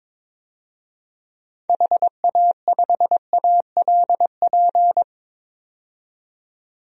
Morse code, heard as a keyed tone, about 22 wpm, 700 Hz. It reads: HA5ALP
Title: DXCC-100 Call Signs - 6 Characters - 22wpm